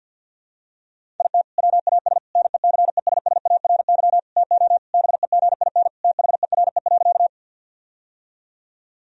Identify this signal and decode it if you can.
Morse code, heard as a keyed tone, about 50 wpm, 700 Hz. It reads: IT WAS NECESSARY TO BEGIN THERE.
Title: Sentences from Top 1000 Words - Review Part 1 - 50wpm